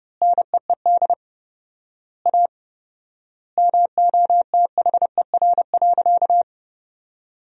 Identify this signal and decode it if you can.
Morse code, heard as a keyed tone, about 30 wpm, 700 Hz. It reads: NEED A MOTHER.
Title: 4x Word-Spacing - Sentences from Top 200 Words - 30wpm